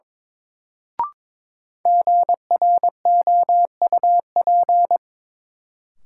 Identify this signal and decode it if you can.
Morse code, heard as a keyed tone, about 22 wpm, 700 Hz. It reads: GROUP
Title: Sentences from Top 1000 Words - Review Part 1 - New Word Drill - 22wpm